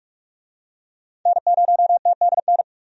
Morse code, heard as a keyed tone, about 43 wpm, 700 Hz. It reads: N0TDN
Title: USA Call Signs - 5 Characters - 45wpm